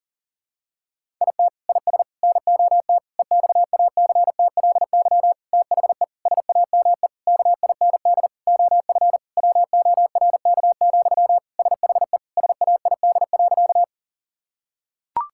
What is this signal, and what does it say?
Morse code, heard as a keyed tone, about 37 wpm, 700 Hz. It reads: IT IS NOT EXACTLY THE SAME KIND OF WORK, SHE SAID.
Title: Sentences from Top 1000 Words - Review Part 6 - 40wpm